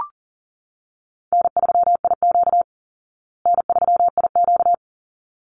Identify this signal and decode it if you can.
Morse code, heard as a keyed tone, about 40 wpm, 700 Hz. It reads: N3IQ N3IQ
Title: Call Signs Repeated 3x - 40wpm